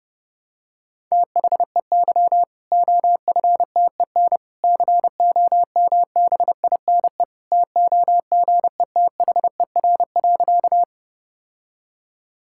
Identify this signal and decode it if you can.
Morse code, heard as a keyed tone, about 29 wpm, 700 Hz. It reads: THEY OFTEN COMBINE TOGETHER.
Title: Sentences from Top 1600 Words 30wpm